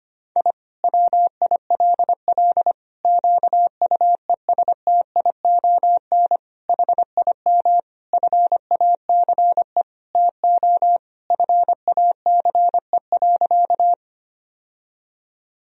Morse code, heard as a keyed tone, about 25 wpm, 700 Hz. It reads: I WILL QUESTION HIM FACE TO FACE.
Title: Sentences from Top 1000 Words - Review Part 2 - 25wpm